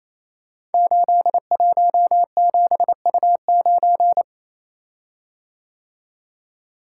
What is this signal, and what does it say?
Morse code, heard as a keyed tone, about 29 wpm, 700 Hz. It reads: 817U9
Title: Code Groups - 5 Letter:Numbers - 28wpm